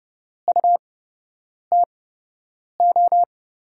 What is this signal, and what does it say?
Morse code, heard as a keyed tone, about 30 wpm, spaced as for 8 wpm, 700 Hz. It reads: UTO
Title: USA Call Signs - 6 Characters - 8x Character Spacing - 30wpm